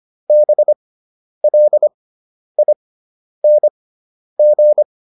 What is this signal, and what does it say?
Morse code, heard as a keyed tone, about 25 wpm, spaced as for 10 wpm, 600 Hz. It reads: BLING
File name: ING5 10